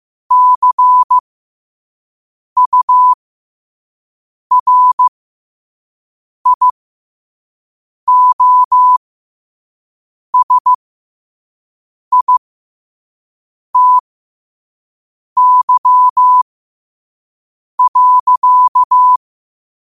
Morse code, heard as a keyed tone, about 15 wpm, spaced as for 5 wpm, 1000 Hz. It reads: CURIOSITY.
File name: qotdfiles / 5wpm